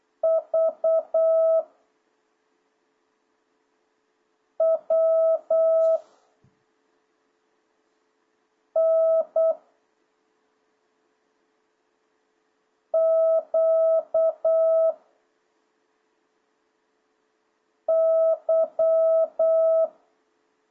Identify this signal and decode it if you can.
Morse code, heard as a keyed tone, about 8 wpm, 660 Hz. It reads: V W N Q Y